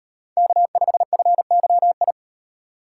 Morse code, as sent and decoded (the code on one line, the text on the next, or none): -.- ..... ..-. -.-- ..
K5FYI